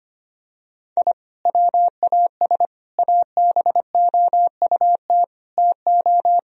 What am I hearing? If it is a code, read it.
Morse code, heard as a keyed tone, about 25 wpm, 700 Hz. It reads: I WAS ABOUT TO
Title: Sentences from Top 1000 Words - Review Part 1 - 25wpm